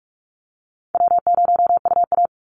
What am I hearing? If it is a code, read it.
Morse code, heard as a keyed tone, about 44 wpm, 700 Hz. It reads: W0UA
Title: Easy Call Signs - 45wpm